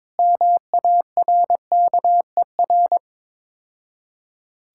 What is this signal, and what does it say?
Morse code, heard as a keyed tone, about 22 wpm, 700 Hz. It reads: MARKER
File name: Easy Words - Speed-Racing with Hints - 22wpm